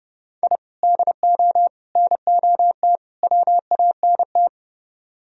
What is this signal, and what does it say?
Morse code, heard as a keyed tone, about 31 wpm, 700 Hz. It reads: I DO NOT WANT